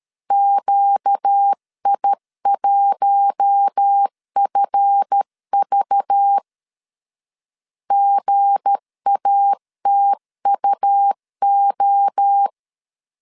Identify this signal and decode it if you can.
Morse code, heard as a keyed tone, about 13 wpm, 800 Hz. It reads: QI1FV GATUO
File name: LEZIONI / 7 GF9Z / 7 GF 12wpm 2122